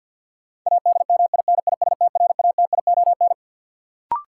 Morse code, heard as a keyed tone, about 47 wpm, 700 Hz. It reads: ADMINISTRATION
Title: Sets of 5 Words - New Word Drill - 50wpm